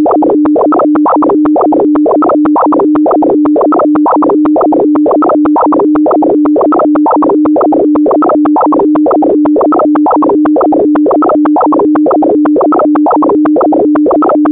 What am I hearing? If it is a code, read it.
Two-tone teleprinter signal, about 170 Hz shift.